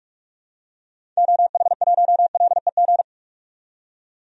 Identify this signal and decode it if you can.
Morse code, heard as a keyed tone, about 46 wpm, 700 Hz. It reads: OH1LEG
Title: DXCC-100 Call Signs - 6 Characters - 45wpm